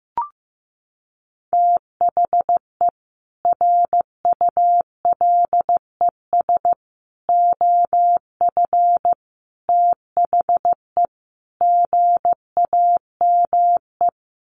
Morse code, heard as a keyed tone, about 15 wpm, 700 Hz. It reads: THE RULES OF THE GAME
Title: Sets of 5 Words Encore2 15wpm